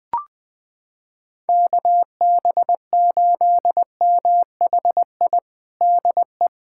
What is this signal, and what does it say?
Morse code, heard as a keyed tone, about 20 wpm, 700 Hz. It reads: KB8MHI DE
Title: USA Callsign DE Callsign 20wpm